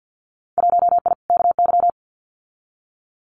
Morse code, heard as a keyed tone, about 50 wpm, 700 Hz. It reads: 1I KY
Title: Winter Field Day Exchanges 50wpm